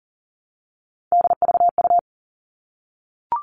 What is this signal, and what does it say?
Morse code, heard as a keyed tone, about 40 wpm, 700 Hz. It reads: DVU
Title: All Cut Numbers - 3-Digit 40wpm